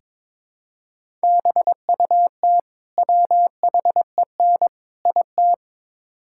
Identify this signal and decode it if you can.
Morse code, heard as a keyed tone, about 22 wpm, 700 Hz. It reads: BUT WHEN IT
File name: practice / 22wpm / Build A Phrase - 5-Word Phrases - 22wpm